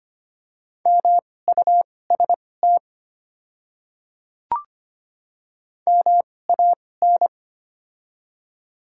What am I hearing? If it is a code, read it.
Morse code, heard as a keyed tone, about 25 wpm, spaced as for 18 wpm, 700 Hz. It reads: MUST MAN
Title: 2x Character Spacing - Top 200 Words - 25wpm